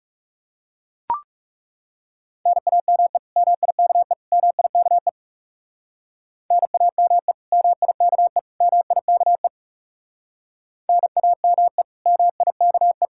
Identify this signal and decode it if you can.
Morse code, heard as a keyed tone, about 38 wpm, 700 Hz. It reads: NAME MIKE MIKE NAME MIKE MIKE NAME MIKE